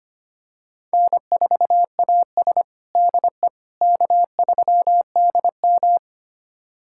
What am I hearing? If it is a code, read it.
Morse code, heard as a keyed tone, about 25 wpm, 700 Hz. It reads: N4AS DE K3DM
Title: USA Callsign DE Callsign - 4 Characters - 25wpm